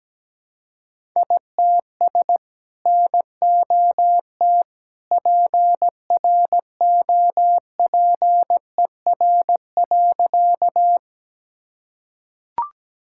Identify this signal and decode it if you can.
Morse code, heard as a keyed tone, about 17 wpm, 700 Hz. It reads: ITS NOT PROPER.
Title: Sentences from Top 1000 Words - Review Part 5 - 17wpm